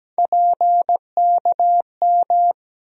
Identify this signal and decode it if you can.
Morse code, heard as a keyed tone, about 17 wpm, 700 Hz. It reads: PKM
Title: USA Call Sign Postfixes - 3 Letters 17wpm